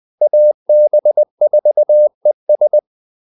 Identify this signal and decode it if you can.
Morse code, heard as a keyed tone, about 20 wpm, 600 Hz. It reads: AB4ES
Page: Kaldesignaler 20 wpm